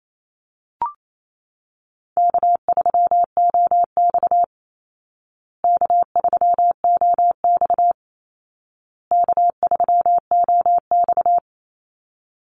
Morse code, heard as a keyed tone, about 28 wpm, 700 Hz. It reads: K3OX K3OX K3OX
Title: Easy Call Signs Repeated 3x - 28wpm